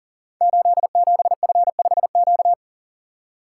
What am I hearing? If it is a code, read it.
Morse code, heard as a keyed tone, about 39 wpm, 700 Hz. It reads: 87F5Q
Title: Code Groups - 5 Letter:Numbers - 40wpm